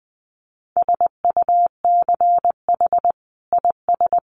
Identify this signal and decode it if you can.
Morse code, heard as a keyed tone, about 19 wpm, 700 Hz. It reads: SUCH IS
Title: Sentences from Top 900 Words 20wpm